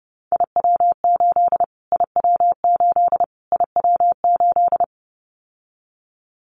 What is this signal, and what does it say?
Morse code, heard as a keyed tone, about 30 wpm, 700 Hz. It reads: IW8 IW8 IW8
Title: Callsign Prefixes for Top 25 DXCC - Repeated 3x - 30wpm